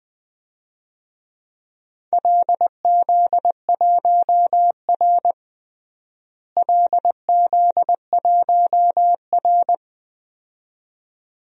Morse code, heard as a keyed tone, about 20 wpm, 700 Hz. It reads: LZ1R LZ1R
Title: DXCC-100 Call Signs - 4 Characters - Repeated 2x - 20wpm